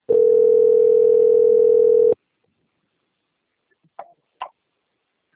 Ringback tone (North America), 440 + 480 Hz.
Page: Toledo - PR